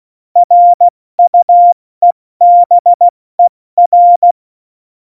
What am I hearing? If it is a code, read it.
Morse code, heard as a keyed tone, about 16 wpm, 700 Hz. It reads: RUEBER
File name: Morsekurs / Witze / Witz 06. Februar 2018, 70 F80